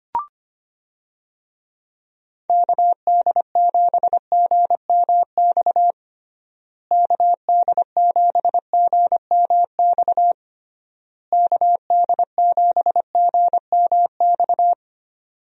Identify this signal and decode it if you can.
Morse code, heard as a keyed tone, about 25 wpm, 700 Hz. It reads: KD7GMX KD7GMX KD7GMX
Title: USA Call Signs - 6 Characters - Repeated 3x - 25wpm